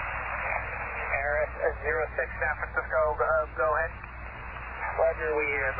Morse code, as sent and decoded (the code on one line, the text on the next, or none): none